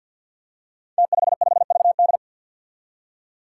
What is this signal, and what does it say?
Morse code, heard as a keyed tone, about 47 wpm, 700 Hz. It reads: T55VD
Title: Five Challenging Letter-Number Combinations 50wpm